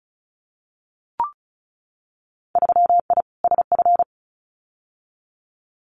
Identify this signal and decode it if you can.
Morse code, heard as a keyed tone, about 34 wpm, 700 Hz. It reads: 3I SF
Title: Winter Field Day Exchanges 35wpm